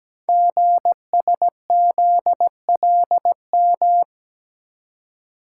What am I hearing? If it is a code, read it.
Morse code, heard as a keyed tone, about 17 wpm, 700 Hz. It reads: GSZLM
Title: Code Groups - 5 Letters - 17wpm